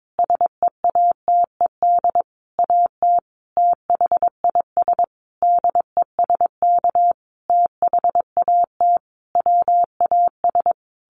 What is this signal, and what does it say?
Morse code, heard as a keyed tone, about 22 wpm, 700 Hz. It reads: SEATED AT THIS DESK THAT WAS
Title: A Sound Of Thunder 22wpm